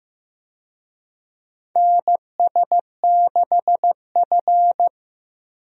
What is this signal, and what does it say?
Morse code, heard as a keyed tone, about 15 wpm, 700 Hz. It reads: NS6F
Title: USA Call Signs - 4 Characters - Repeated 3x - 15wpm